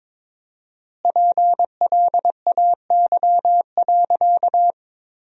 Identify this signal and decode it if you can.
Morse code, heard as a keyed tone, about 22 wpm, 700 Hz. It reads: PLAY.
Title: 4x Word-Spacing - Sentences from Top 200 Words - 22wpm